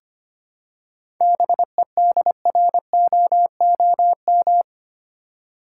Morse code, heard as a keyed tone, about 25 wpm, 700 Hz. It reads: BEDROOM
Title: Seven Letter Words - 25wpm